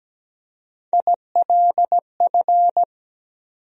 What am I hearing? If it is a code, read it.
Morse code, heard as a keyed tone, about 17 wpm, 700 Hz. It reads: ILF
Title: USA Call Sign Postfixes - 3 Letters 17wpm